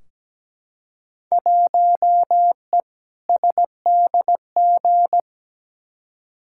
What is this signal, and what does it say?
Morse code, heard as a keyed tone, about 17 wpm, 700 Hz. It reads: 1E SDG
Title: Field Day Exchanges - 17wpm